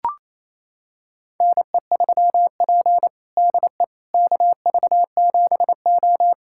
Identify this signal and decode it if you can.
Morse code, heard as a keyed tone, about 28 wpm, 700 Hz. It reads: NE3P DE KV7O
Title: USA Callsign DE Callsign - 4 Characters - 28wpm